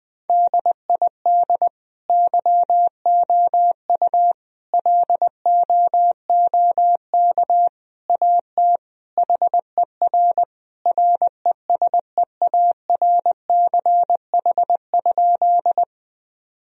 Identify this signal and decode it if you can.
Morse code, heard as a keyed tone, about 20 wpm, 700 Hz. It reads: DID YOU LOOK AT HER RESEARCH?
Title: Sentences from Top 1000 Words - Review Part 4 - 20wpm